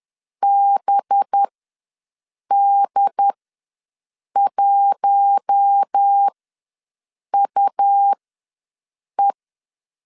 Morse code, heard as a keyed tone, about 10 wpm, spaced as for 6 wpm, 800 Hz. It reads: BD1UE
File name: LEZIONI / 5 LBVC / 5 LB ws 10wpm 3616